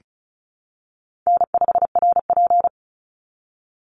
Morse code, heard as a keyed tone, about 34 wpm, 700 Hz. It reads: N5RP
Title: Call Signs Repeated 3x - 35wpm